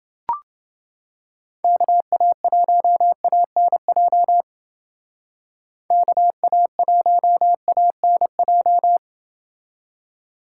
Morse code, heard as a keyed tone, about 29 wpm, 700 Hz. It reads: KA1ANJ KA1ANJ